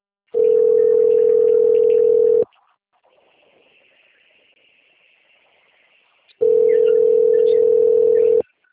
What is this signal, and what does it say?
Ringback tone (North America), 440 + 480 Hz.